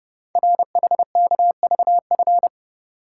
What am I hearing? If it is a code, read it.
Morse code, heard as a keyed tone, about 30 wpm, 700 Hz. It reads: RHKVF
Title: Code Groups - 5 Letters - 30wpm